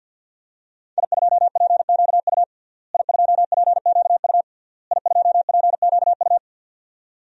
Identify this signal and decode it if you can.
Morse code, heard as a keyed tone, about 52 wpm, 700 Hz. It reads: I2PQU I2PQU I2PQU
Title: DXCC-100 Call Signs - 5 Characters - Repeated 3x - 50wpm